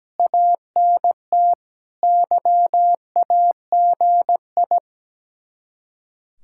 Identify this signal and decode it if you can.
Morse code, heard as a keyed tone, about 17 wpm, 700 Hz. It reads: ANT YAGI
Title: Speed-Racing Common QSO Phrases - 17wpm